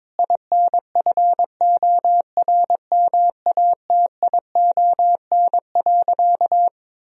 Morse code, heard as a keyed tone, about 22 wpm, 700 Hz. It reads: INFORMATION.
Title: Sentences from Top 1000 Words - Review Part 4 - 22wpm